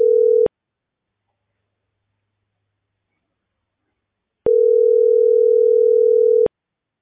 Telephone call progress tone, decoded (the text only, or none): ringback tone